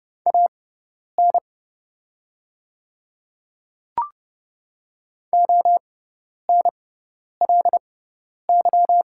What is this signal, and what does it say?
Morse code, heard as a keyed tone, about 30 wpm, spaced as for 10 wpm, 700 Hz. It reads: AN ONLY
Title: 6x Character Spacing - Top 300 Words - 30wpm